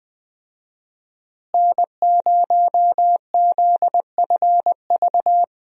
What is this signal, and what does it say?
Morse code, heard as a keyed tone, about 20 wpm, 700 Hz. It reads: N0ZFV
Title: USA Call Signs - 5 Characters - Repeated 2x - 20wpm